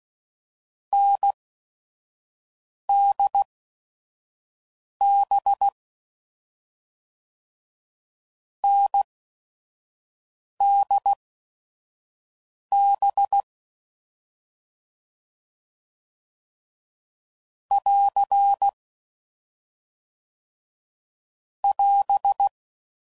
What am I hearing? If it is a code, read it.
Morse code, heard as a keyed tone, about 16 wpm, spaced as for 5 wpm, 800 Hz. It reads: NDB NDB <AR> <AS>
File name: NDB